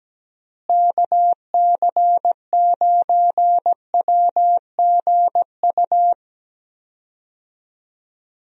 Morse code, heard as a keyed tone, about 17 wpm, 700 Hz. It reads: KC9WGU
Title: USA Call Signs - 6 Characters - Repeated 3x - 17wpm